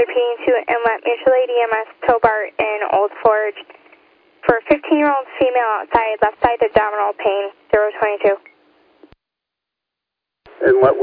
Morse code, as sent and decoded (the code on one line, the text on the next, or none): none